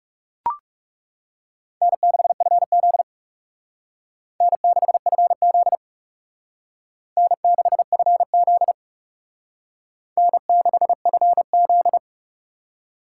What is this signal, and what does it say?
Morse code, heard as a keyed tone, about 37 wpm, 700 Hz. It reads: N6FZ N6FZ N6FZ N6FZ
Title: Speed Racing - USA 1x2 Call Signs 30wpm